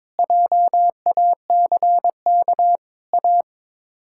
Morse code, heard as a keyed tone, about 22 wpm, 700 Hz. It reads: JACK A
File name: practice / 22wpm / Sets of 7 Words - 22wpm